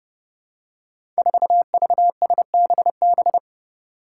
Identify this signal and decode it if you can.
Morse code, heard as a keyed tone, about 28 wpm, 700 Hz. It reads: 4VSBB
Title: Five Challenging Letter-Number Combinations 30wpm